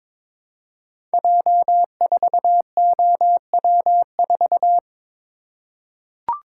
Morse code, heard as a keyed tone, about 22 wpm, 700 Hz. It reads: J4OW4
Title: Code Groups - 5 Letter:Numbers - 22wpm